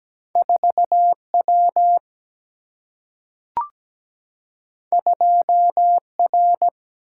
Morse code, heard as a keyed tone, about 17 wpm, 700 Hz. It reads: 4W 2R
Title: Single-Letter Single-Number 17wpm